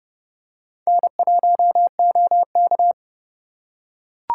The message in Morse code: N1OK

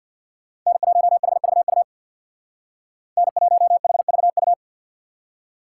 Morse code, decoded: N1HVU N1HVU

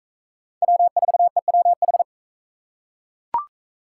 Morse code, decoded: W4EWH

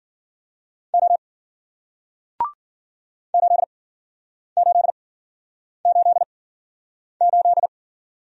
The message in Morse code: O 8888

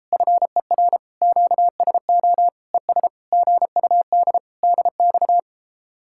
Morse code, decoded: FER QSO ES GUD DX

